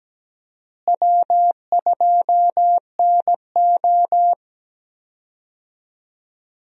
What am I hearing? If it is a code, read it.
Morse code, heard as a keyed tone, about 17 wpm, 700 Hz. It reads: W2NO
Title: USA Call Signs - 4 Characters - Repeated 2x - 17wpm